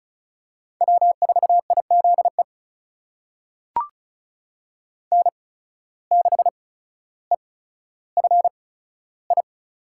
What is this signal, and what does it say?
Morse code, heard as a keyed tone, about 36 wpm, spaced as for 15 wpm, 700 Hz. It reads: W4IZE N6EFI
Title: USA Call Signs - 5 Characters - 8x Character Spacing - 35wpm